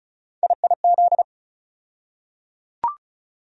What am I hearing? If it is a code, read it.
Morse code, heard as a keyed tone, about 36 wpm, 700 Hz. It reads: IIZ
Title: USA Call Sign Postfixes - 3 Letters 35wpm